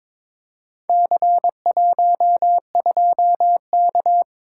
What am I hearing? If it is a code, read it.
Morse code, heard as a keyed tone, about 22 wpm, 700 Hz. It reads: C12K
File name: practice / 22wpm / Four Letter-Number Combinations - 22wpm